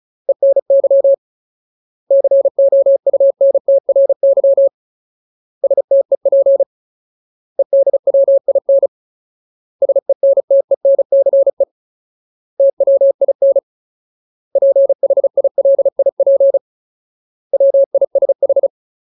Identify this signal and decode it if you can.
Morse code, heard as a keyed tone, about 34 wpm, 550 Hz. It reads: ENY COUNTRY STEP EDWIN SENTENCE TWIN PHILIP WISH
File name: Ultimate Word List / 35WPM Ultimate